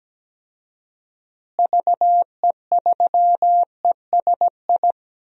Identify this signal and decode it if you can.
Morse code, heard as a keyed tone, about 17 wpm, 700 Hz. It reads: VE3ESI
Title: DXCC-100 Call Signs - 6 Characters - Repeated 2x - 17wpm